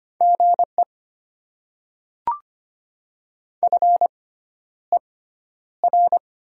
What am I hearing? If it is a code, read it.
Morse code, heard as a keyed tone, about 25 wpm, spaced as for 11 wpm, 700 Hz. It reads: GE FER
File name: Top 20 QSO Elements - 6x Character Spacing - 25wpm